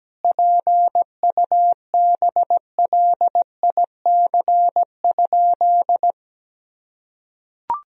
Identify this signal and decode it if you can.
Morse code, heard as a keyed tone, about 17 wpm, 700 Hz. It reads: PUBLIC?
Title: Sentences from Top 1000 Words - Review Part 3 - 17wpm